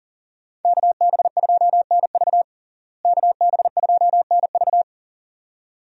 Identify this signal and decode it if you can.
Morse code, heard as a keyed tone, about 40 wpm, 700 Hz. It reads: KB2NV KB2NV
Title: USA Call Signs - 5 Characters - Repeated 2x - 40wpm